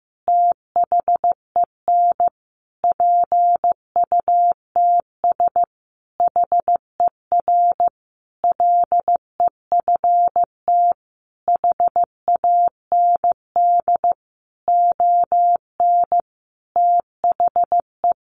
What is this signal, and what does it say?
Morse code, heard as a keyed tone, about 15 wpm, 700 Hz. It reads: THEN PUTS HER LEFT HAND ON THE
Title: Easy Phrases 15wpm